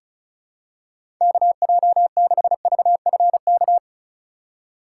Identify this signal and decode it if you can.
Morse code, heard as a keyed tone, about 36 wpm, 700 Hz. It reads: KJ6VFK
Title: USA Call Signs - 6 Characters - 4x Character Spacing - 35wpm